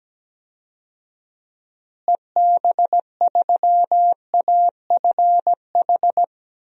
Morse code, heard as a keyed tone, about 17 wpm, 700 Hz. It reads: EB3AFH
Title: DXCC-100 Call Signs - 6 Characters - Repeated 2x - 17wpm